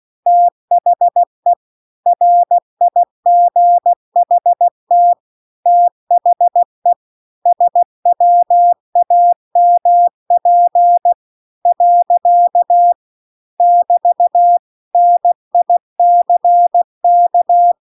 Morse code, heard as a keyed tone, about 16 wpm, 700 Hz. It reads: THE RIGHT THE SWAMP . <BT> NICK